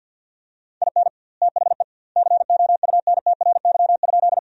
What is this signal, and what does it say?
Morse code, heard as a keyed tone, about 53 wpm, 700 Hz. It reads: IN THE COUNTRY?
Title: 2x Word-Spacing - Sentences from Top 300 Words - 50wpm